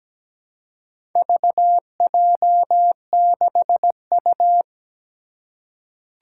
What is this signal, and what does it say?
Morse code, heard as a keyed tone, about 17 wpm, 700 Hz. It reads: VJ6U
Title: Four Challenging Letter-Number Combinations 17wpm